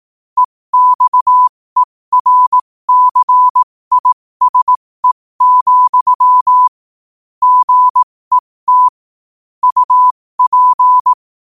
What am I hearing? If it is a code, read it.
Morse code, heard as a keyed tone, about 18 wpm, spaced as for 15 wpm, 1000 Hz. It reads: EXERCISE, GET UP
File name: qotdfiles / 15wpm